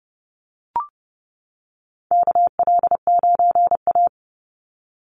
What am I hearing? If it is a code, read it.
Morse code, heard as a keyed tone, about 29 wpm, 700 Hz. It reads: KL9A
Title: Call Signs - 30wpm